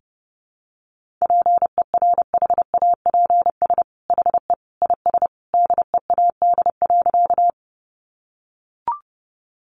Morse code, heard as a keyed tone, about 30 wpm, 700 Hz. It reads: PERHAPS HE IS DEAD.
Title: Sentences from Top 1500 Words 30wpm